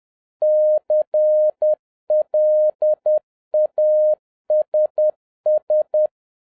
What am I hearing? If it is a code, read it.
Morse code, heard as a keyed tone, about 10 wpm, 610 Hz. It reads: CLASS